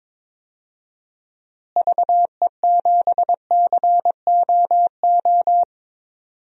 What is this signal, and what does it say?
Morse code, heard as a keyed tone, about 22 wpm, 700 Hz. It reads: VE7COO
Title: DXCC-100 Call Signs - 6 Characters - Repeated 3x - 22wpm